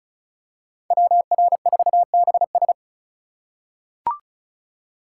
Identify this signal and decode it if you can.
Morse code, heard as a keyed tone, about 34 wpm, 700 Hz. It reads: WR4BS
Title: USA Call Signs - 5 Characters - 4x Character Spacing - 35wpm